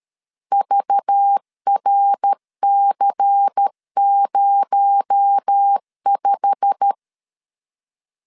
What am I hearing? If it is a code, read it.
Morse code, heard as a keyed tone, about 13 wpm, 800 Hz. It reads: VRC05
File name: LEZIONI / 8 YJ45 / 8 YJ45 12wpm 2151